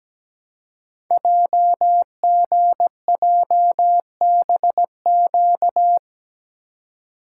Morse code, heard as a keyed tone, about 17 wpm, 700 Hz. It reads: JGJBQ
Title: Code Groups - 5 Letters - 17wpm